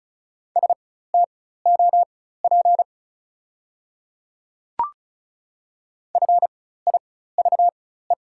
Morse code, heard as a keyed tone, about 35 wpm, spaced as for 16 wpm, 690 Hz. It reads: STOP FIVE
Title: 4x Character Spacing - Top 500 Words - 35wpm